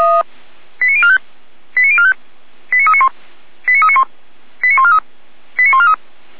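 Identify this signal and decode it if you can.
Five-tone selective call (ZVEI), heard as quick runs of five tones, tones of about 70 ms. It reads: 89045 89045 89391 89391 89143 89143
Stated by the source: ZVEI five-tone call received by SDR.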